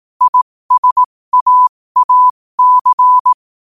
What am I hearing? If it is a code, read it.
Morse code, heard as a keyed tone, about 18 wpm, spaced as for 15 wpm, 1000 Hz. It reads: ISAAC
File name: qotdfiles / 15wpm